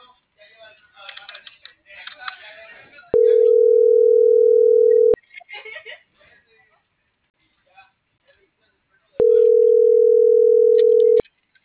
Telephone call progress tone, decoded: ringback tone

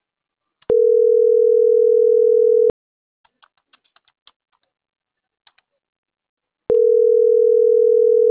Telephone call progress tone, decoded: ringback tone

